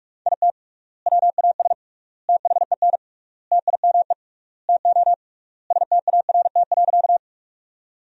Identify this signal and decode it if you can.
Morse code, heard as a keyed tone, about 44 wpm, 700 Hz. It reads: IT WAS THEN TIME TO START.
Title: 3x Word-Spacing - Sentences from Top 300 Words - 45wpm